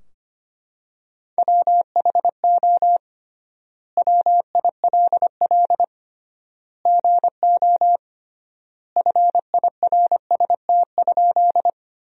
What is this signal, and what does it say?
Morse code, heard as a keyed tone, about 25 wpm, 700 Hz. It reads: WHO WILL GO FIRST?
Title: 3x Word-Spacing - Sentences from Top 100 Words - 25wpm